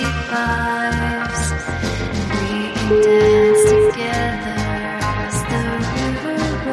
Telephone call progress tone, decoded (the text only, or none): ringback tone